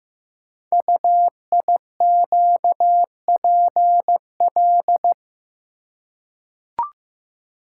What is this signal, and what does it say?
Morse code, heard as a keyed tone, about 15 wpm, 700 Hz. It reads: UIQPL
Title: Code Groups - 5 Letters - 15wpm